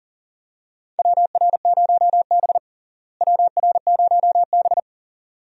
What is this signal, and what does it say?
Morse code, heard as a keyed tone, about 37 wpm, 700 Hz. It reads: WR0B WR0B
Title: USA Call Signs - 4 Characters - Repeated 2x - 40wpm